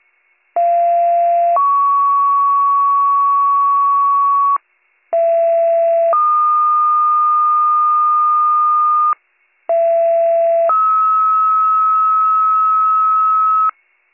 Two-tone page, heard as a one-second tone, then a three-second tone, about 690 then 1120 Hz, about 670 then 1220 Hz, about 670 then 1290 Hz. It is heard from 0.6 to 13.7 s.